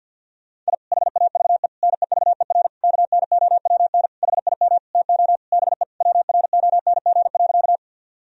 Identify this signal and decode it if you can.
Morse code, heard as a keyed tone, about 50 wpm, 700 Hz. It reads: I HAVE NEVER KNOWN HIM TO BE WRONG.
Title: Sentences from Top 1000 Words - Review Part 3 - 50wpm